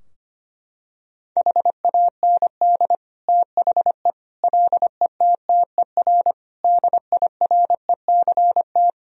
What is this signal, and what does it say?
Morse code, heard as a keyed tone, about 25 wpm, 700 Hz. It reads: HAND THE LETTER DIRECT
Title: Sentences from Top 1000 Words - Review Part 2 - 25wpm